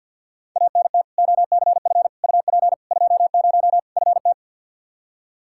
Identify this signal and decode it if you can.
Morse code, heard as a keyed tone, about 52 wpm, 700 Hz. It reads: ANT OCF UP 20 FT
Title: Practice QSO - Exchange 2 - Info - 50wpm